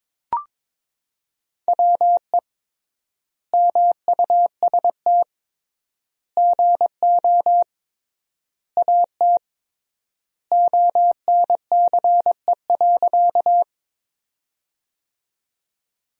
Morse code, heard as a keyed tone, about 22 wpm, 700 Hz. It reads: WE MUST GO AT ONCE.
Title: 3x Word-Spacing - Sentences from Top 300 Words - 22wpm